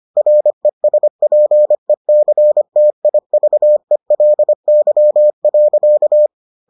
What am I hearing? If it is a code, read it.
Morse code, heard as a keyed tone, about 25 wpm, 600 Hz. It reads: RESPECTIVELY.